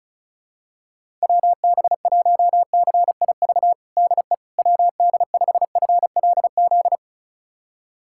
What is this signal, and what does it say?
Morse code, heard as a keyed tone, about 36 wpm, 700 Hz. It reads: WB1CIV DE WD5FLZ
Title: USA Callsign DE Callsign - 6 Characters - 35wpm